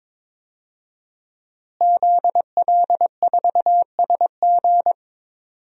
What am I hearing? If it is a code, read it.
Morse code, heard as a keyed tone, about 22 wpm, 700 Hz. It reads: ZL4SG
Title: DXCC-100 Call Signs - 5 Characters - Repeated 3x - 22wpm